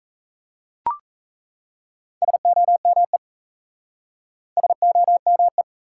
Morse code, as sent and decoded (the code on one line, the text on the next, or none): ... --- -- . / ... --- -- .
SOME SOME